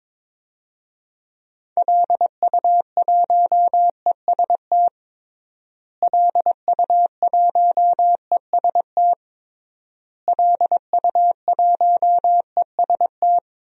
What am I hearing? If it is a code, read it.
Morse code, heard as a keyed tone, about 22 wpm, 700 Hz. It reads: LU1EST LU1EST LU1EST
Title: DXCC-100 Call Signs - 6 Characters - Repeated 3x - 22wpm